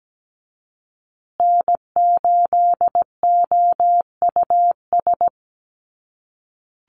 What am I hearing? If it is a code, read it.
Morse code, heard as a keyed tone, about 17 wpm, 700 Hz. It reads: N8OUS
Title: USA Call Signs Repeated 3x 17wpm